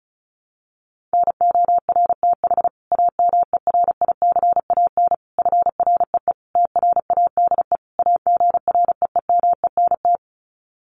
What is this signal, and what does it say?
Morse code, heard as a keyed tone, about 34 wpm, 700 Hz. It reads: NORTH AMERICAN FREE TRADE AGREEMENT
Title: Sets of 5 Words Encore 35wpm